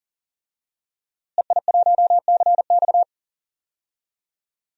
Morse code, heard as a keyed tone, about 39 wpm, 700 Hz. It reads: EI1CX
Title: DXCC-100 Call Signs - 5 Characters - Repeated 2x - 40wpm